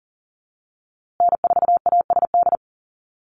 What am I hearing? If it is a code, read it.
Morse code, heard as a keyed tone, about 40 wpm, 700 Hz. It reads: N4ASD
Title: USA Call Signs Repeated 2x 40wpm